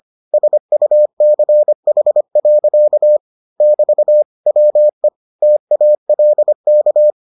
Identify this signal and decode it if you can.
Morse code, heard as a keyed tone, about 25 wpm, 600 Hz. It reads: SUCH. <BT> WE TALK